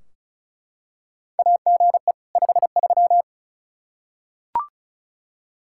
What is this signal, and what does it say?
Morse code, heard as a keyed tone, about 36 wpm, 700 Hz. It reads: AGE 53
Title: K1USN SST QSO Practice 35wpm